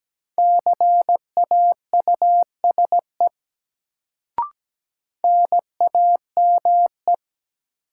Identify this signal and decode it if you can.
Morse code, heard as a keyed tone, about 17 wpm, 700 Hz. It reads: CAUSE NAME
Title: Sentences from Top 200 Words - New Word Drill - 17wpm